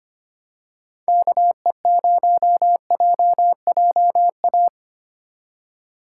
Morse code, heard as a keyed tone, about 25 wpm, 700 Hz. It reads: KE0JJA
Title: USA Call Signs - 6 Characters - 3x Character Spacing - 25wpm